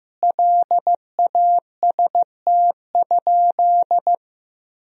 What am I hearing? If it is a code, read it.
Morse code, heard as a keyed tone, about 15 wpm, 700 Hz. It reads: LAST?
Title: Sentences from Top 1000 Words - Review Part 4 - 15wpm